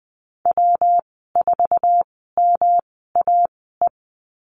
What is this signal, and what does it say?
Morse code, heard as a keyed tone, about 20 wpm, spaced as for 14 wpm, 700 Hz. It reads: W4MAE
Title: 140 Lesson 23g - Letter f - Call Signs 20wpm